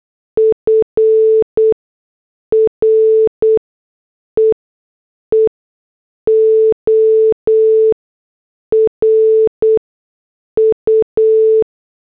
Morse code, heard as a keyed tone, about 8 wpm, spaced as for 6 wpm, 440 Hz. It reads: FREE OR U
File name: morse / headlines